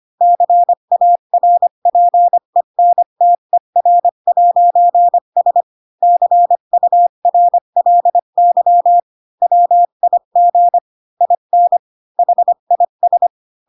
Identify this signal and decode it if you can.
Morse code, heard as a keyed tone, about 25 wpm, 700 Hz. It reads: CARPENTER'S CURLY WIG IN HIS